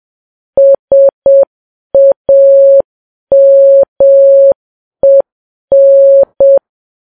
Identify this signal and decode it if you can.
Morse code, heard as a keyed tone, about 7 wpm, 550 Hz. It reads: SAMEN
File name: Files Web tls / Morse on9fv / morse7kps